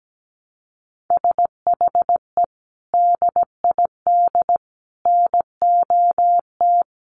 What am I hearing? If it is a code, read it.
Morse code, heard as a keyed tone, about 17 wpm, 700 Hz. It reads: SHE DID NOT
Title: Sentences from Top 600 Words 17wpm